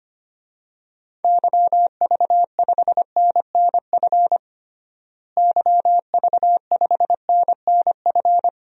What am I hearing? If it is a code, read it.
Morse code, heard as a keyed tone, about 25 wpm, 700 Hz. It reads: YV5NNF YV5NNF
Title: DXCC-100 Call Signs - 6 Characters - Repeated 3x - 25wpm